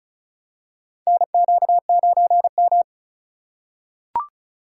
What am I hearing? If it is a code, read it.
Morse code, heard as a keyed tone, about 35 wpm, 700 Hz. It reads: NQ9M
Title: USA Call Signs - 4 Characters - 2x Character Spacing - 35wpm